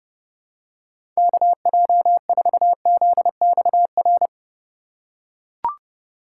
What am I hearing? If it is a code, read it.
Morse code, heard as a keyed tone, about 30 wpm, 700 Hz. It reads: KJ4ZXR
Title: USA Call Signs - 6 Characters - 3x Character Spacing - 30wpm